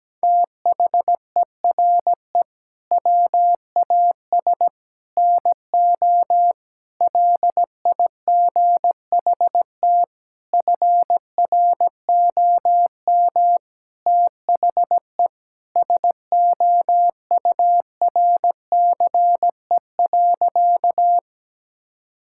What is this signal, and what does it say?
Morse code, heard as a keyed tone, about 17 wpm, 700 Hz. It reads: THERE WAS NO LIGHT FROM THE SOURCE.
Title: Sentences from Top 1000 Words - Review Part 5 - 17wpm